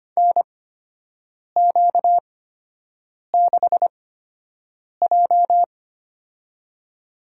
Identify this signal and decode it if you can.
Morse code, heard as a keyed tone, about 25 wpm, spaced as for 7 wpm, 700 Hz. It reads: NQ6J
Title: USA Call Signs - 4 Characters - 8x Character Spacing - 25wpm